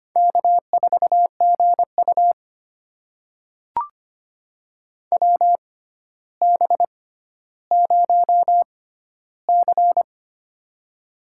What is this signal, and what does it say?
Morse code, heard as a keyed tone, about 25 wpm, spaced as for 13 wpm, 700 Hz. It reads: K4GU WB0C
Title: USA Call Signs - 6x Character Spacing - 25wpm